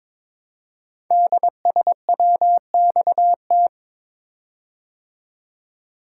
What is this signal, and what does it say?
Morse code, heard as a keyed tone, about 22 wpm, 700 Hz. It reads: DSWXT
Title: Code Groups - 5 Letters - 22wpm